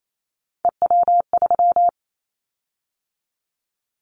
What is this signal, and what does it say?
Morse code, heard as a keyed tone, about 28 wpm, 700 Hz. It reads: EW3
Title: Callsign Prefixes for Top 100 DXCC - 28wpm